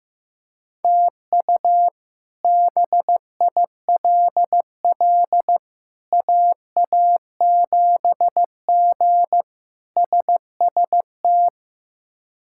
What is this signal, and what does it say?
Morse code, heard as a keyed tone, about 15 wpm, 700 Hz. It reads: TU BILL AA7G SST
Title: K1USN SST QSO Practice - 15wpm